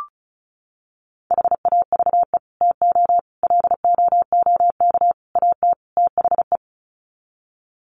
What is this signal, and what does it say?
Morse code, heard as a keyed tone, about 34 wpm, 700 Hz. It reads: HAVE TO LOOK AT THE
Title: Sets of 5 Words Encore 35wpm